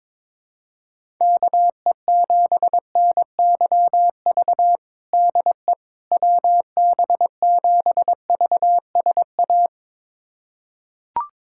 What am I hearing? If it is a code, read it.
Morse code, heard as a keyed tone, about 21 wpm, 700 Hz. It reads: KE7NYV DE WB7VSA
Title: USA Callsign DE Callsign 22wpm